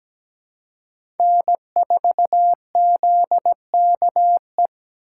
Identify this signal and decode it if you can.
Morse code, heard as a keyed tone, about 17 wpm, 700 Hz. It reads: N4ZKE
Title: USA Call Signs - 5 Characters - 4x Character Spacing - 17wpm